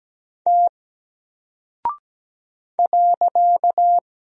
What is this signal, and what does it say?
Morse code, heard as a keyed tone, about 17 wpm, 700 Hz. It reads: T .
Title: Single Character (Letter:Number:Punctuation:Prosign) Alt 17wpm